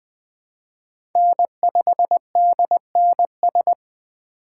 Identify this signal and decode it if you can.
Morse code, heard as a keyed tone, about 21 wpm, 700 Hz. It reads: N5DNS
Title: USA Call Signs - 5 Characters - 4x Character Spacing - 20wpm